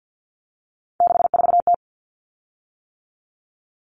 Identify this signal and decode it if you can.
Morse code, heard as a keyed tone, about 50 wpm, 700 Hz. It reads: 64T
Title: Common Cut Numbers - 3-Digit 50wpm